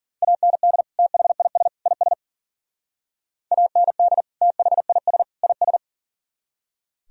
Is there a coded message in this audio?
Yes: Morse code from 0.2 to 5.8 s, heard as a keyed tone, about 52 wpm, 700 Hz. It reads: AND THIS IS AND THIS IS